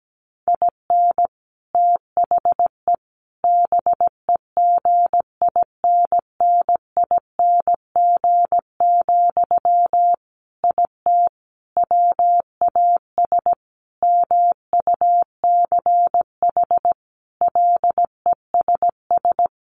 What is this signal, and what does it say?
Morse code, heard as a keyed tone, about 17 wpm, 700 Hz. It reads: IN THE BEGINNING, IT WAS MUCH LESS
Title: Sentences from Top 1400 Words 17wpm